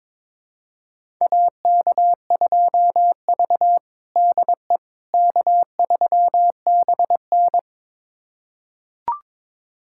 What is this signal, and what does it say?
Morse code, heard as a keyed tone, about 22 wpm, 700 Hz. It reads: AK2V DE K3BN
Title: USA Callsign DE Callsign - 4 Characters - 22wpm